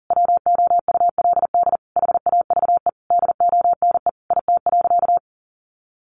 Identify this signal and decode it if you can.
Morse code, heard as a keyed tone, about 40 wpm, 700 Hz. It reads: WOULD HAVE DONE IT.
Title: Sentences from Top 1200 Words 40wpm